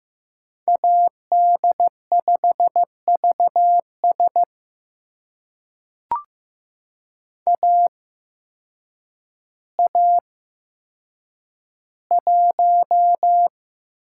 Morse code, heard as a keyed tone, about 15 wpm, spaced as for 8 wpm, 700 Hz. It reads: AD5VS AA1
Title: USA Call Signs - 8x Character Spacing - 15wpm